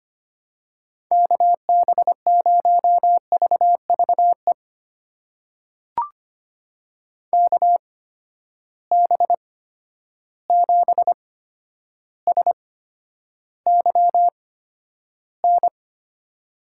Morse code, heard as a keyed tone, about 25 wpm, spaced as for 11 wpm, 700 Hz. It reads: KB0VVE KB7SYN
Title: USA Call Signs - 6 Characters - 8x Character Spacing - 25wpm